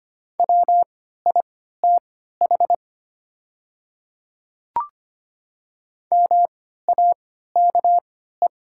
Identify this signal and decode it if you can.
Morse code, heard as a keyed tone, about 25 wpm, spaced as for 14 wpm, 700 Hz. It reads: WITH MAKE
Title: 3x Character Spacing - Top 100 Words - 25wpm